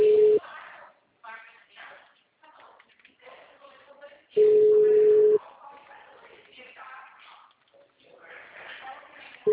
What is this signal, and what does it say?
Ringback tone (Europe, 425 Hz), with speech.